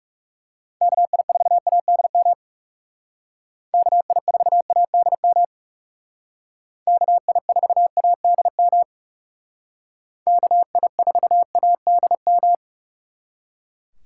Morse code, heard as a keyed tone, about 37 wpm, 700 Hz. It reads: KI4ADM KI4ADM KI4ADM KI4ADM
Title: Speed Racing - USA 2x3 Call Signs 30wpm